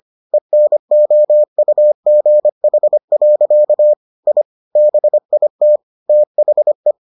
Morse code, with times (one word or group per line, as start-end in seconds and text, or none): 0.3-3.9 ENOUGH.
4.3-4.4 I
4.8-5.8 BIT
6.1-6.9 THE